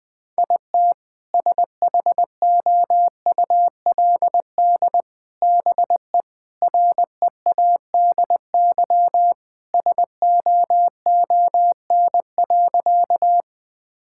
Morse code, heard as a keyed tone, about 21 wpm, 700 Hz. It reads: IT SHOULD BE READY SOON.
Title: Sentences with 2-Word Phrases - Spoken Beginning - 20wpm